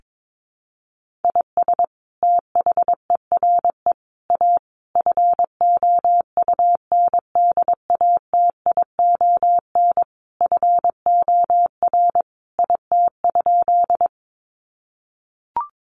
Morse code, heard as a keyed tone, about 22 wpm, 700 Hz. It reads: IS THERE A FOUNDATION FOR IT?
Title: Sentences from Top 1200 Words 22wpm